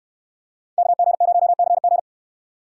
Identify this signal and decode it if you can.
Morse code, heard as a keyed tone, about 69 wpm, 700 Hz. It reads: KG0CG